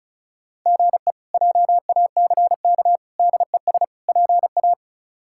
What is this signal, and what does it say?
Morse code, heard as a keyed tone, about 36 wpm, 700 Hz. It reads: GE JACK DES PA
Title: K1USN SST QSO Practice - 35wpm